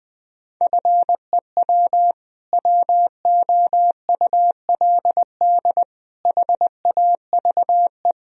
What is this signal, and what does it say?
Morse code, heard as a keyed tone, about 20 wpm, 700 Hz. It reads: FEW WOULD HAVE
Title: Sentences from Top 1000 Words - Review Part 1 - 20wpm